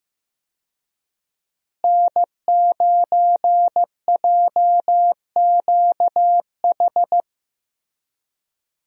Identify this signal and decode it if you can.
Morse code, heard as a keyed tone, about 15 wpm, 700 Hz. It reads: N9JQH
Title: USA Call Signs - 5 Characters - Repeated 2x - 15wpm